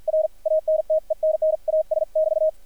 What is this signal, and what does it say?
Morse code, heard as a keyed tone, about 48 wpm, 600 Hz, with radio noise. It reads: R ANTENNAS <BT>
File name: archived / CRSnet